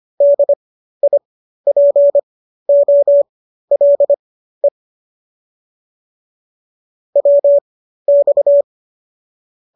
Morse code, heard as a keyed tone, about 25 wpm, spaced as for 13 wpm, 570 Hz. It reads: DIPOLE WX